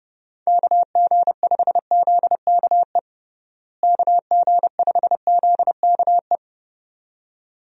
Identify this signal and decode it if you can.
Morse code, heard as a keyed tone, about 31 wpm, 700 Hz. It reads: KG5ZKE KG5ZKE
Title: USA Call Signs - 6 Characters - Repeated 2x - 30wpm